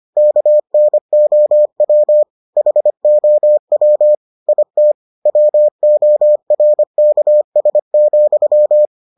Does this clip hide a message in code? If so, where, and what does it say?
Yes: Morse code from 0.2 to 8.9 s, heard as a keyed tone, about 25 wpm, 600 Hz. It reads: KNOW HOW IT WORKS,